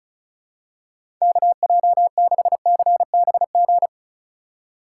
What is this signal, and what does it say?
Morse code, heard as a keyed tone, about 36 wpm, 700 Hz. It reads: KJ6CBG
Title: USA Call Signs - 6 Characters - 2x Character Spacing - 35wpm